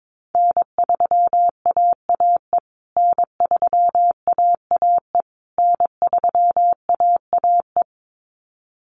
Morse code, heard as a keyed tone, about 22 wpm, 700 Hz. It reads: N3AAE N3AAE N3AAE
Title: USA Call Signs Repeated 3x 22wpm